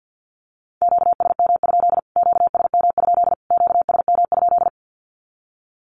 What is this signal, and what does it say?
Morse code, heard as a keyed tone, about 50 wpm, 700 Hz. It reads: QSM? QSM? QSM?
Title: Q-Codes Repeated 3x 50wpm